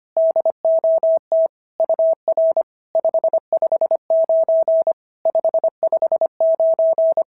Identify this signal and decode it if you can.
Morse code, heard as a keyed tone, about 25 wpm, 650 Hz. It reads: DOT UR 559 559
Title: SOTA QSO Practice 25wpm